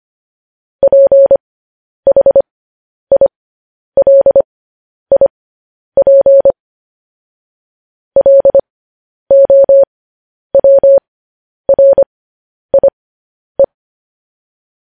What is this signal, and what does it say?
Morse code, heard as a keyed tone, about 25 wpm, spaced as for 10 wpm, 550 Hz. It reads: PHILIP LOWRIE
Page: news headlines in morse code at 10 wpm